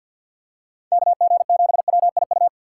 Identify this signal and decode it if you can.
Morse code, heard as a keyed tone, about 50 wpm, 700 Hz. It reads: KG7WIU
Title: USA Call Signs - 6 Characters - 6x Character Spacing - 50wpm